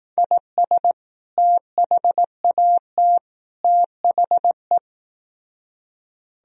Morse code, heard as a keyed tone, about 18 wpm, 700 Hz. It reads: IS THAT THE
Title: Speed-Racing - Sets of 3 Words 15wpm